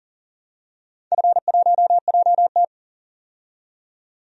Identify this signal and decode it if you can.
Morse code, heard as a keyed tone, about 40 wpm, 700 Hz. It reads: F1JT
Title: DXCC-100 Call Signs - 4 Characters - 40wpm